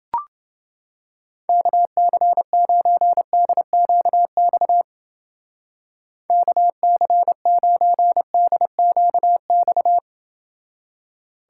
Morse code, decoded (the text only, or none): KC9DQX KC9DQX